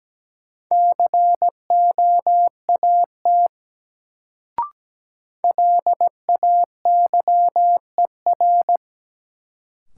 Morse code, heard as a keyed tone, about 17 wpm, 700 Hz. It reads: COAT LAYER
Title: Sentences from Top 1600 Words - New Word Drill - 17wpm